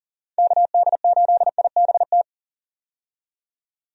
Morse code, heard as a keyed tone, about 39 wpm, 700 Hz. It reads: KD8IBT
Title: USA Call Signs - 6 Characters - 40wpm